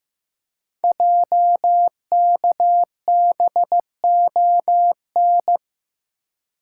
Morse code, heard as a keyed tone, about 15 wpm, 700 Hz. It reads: JKBON
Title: Code Groups - 5 Letter:Numbers - 15wpm